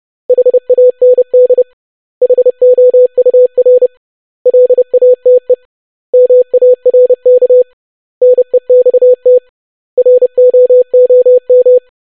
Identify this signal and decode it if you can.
Morse code, heard as a keyed tone, about 30 wpm, 500 Hz. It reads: HAND HOUR LATE MARK NEXT ROOM
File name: basic500-25wpm